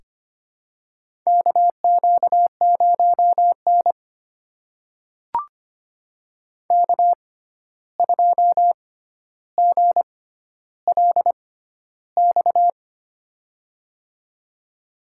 Morse code, heard as a keyed tone, about 25 wpm, spaced as for 12 wpm, 700 Hz. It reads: KQ0N K2GLX
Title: USA Call Signs - 6x Character Spacing - 25wpm